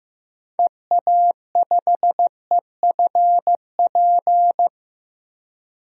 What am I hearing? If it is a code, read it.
Morse code, heard as a keyed tone, about 15 wpm, 700 Hz. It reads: EA5EFP
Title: DXCC-100 Call Signs - 6 Characters - Repeated 2x - 15wpm